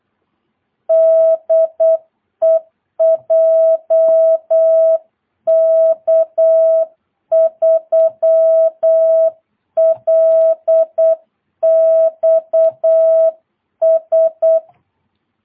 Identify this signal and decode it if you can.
Morse code, heard as a keyed tone, about 8 wpm, 660 Hz. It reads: DE JK3LXS
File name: jh8cbh / 2022cw